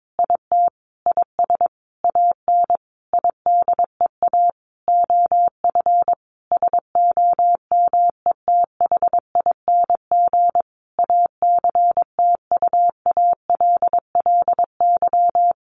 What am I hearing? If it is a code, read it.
Morse code, heard as a keyed tone, about 22 wpm, 700 Hz. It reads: IT IS AN IDEA OF SOMETHING ACTUALLY
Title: Sentences from Top 1500 Words 22wpm